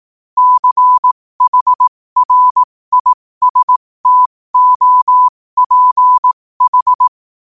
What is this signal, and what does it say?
Morse code, heard as a keyed tone, about 18 wpm, spaced as for 15 wpm, 1000 Hz. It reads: CHRISTOPH